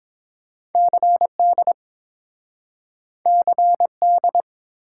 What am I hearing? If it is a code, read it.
Morse code, heard as a keyed tone, about 30 wpm, 700 Hz. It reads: CD CD